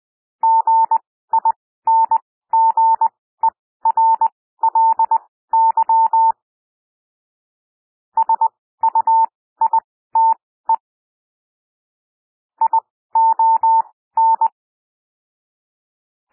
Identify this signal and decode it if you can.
Morse code, heard as a keyed tone, about 20 wpm, spaced as for 14 wpm, 900 Hz. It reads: GINGERLY SUITE ION
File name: SOUND WORDS 20 wpm